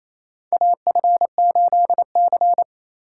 Morse code, heard as a keyed tone, about 28 wpm, 700 Hz. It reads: AF8C
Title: USA Call Signs - 4 Characters - 4x Character Spacing - 28wpm